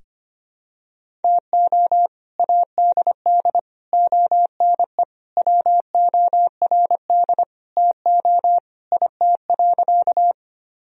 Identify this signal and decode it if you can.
Morse code, heard as a keyed tone, about 25 wpm, 700 Hz. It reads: TO ADD ONE WORD TO IT.
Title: Sentences from Top 1000 Words - Review Part 1 - 25wpm